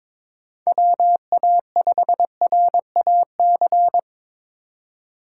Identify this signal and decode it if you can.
Morse code, heard as a keyed tone, about 22 wpm, 700 Hz. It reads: WA5RAC
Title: USA Call Signs - 6 Characters - 2x Character Spacing - 22wpm